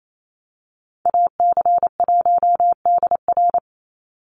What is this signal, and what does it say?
Morse code, heard as a keyed tone, about 28 wpm, 700 Hz. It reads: AC1DR
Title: USA Call Signs 28wpm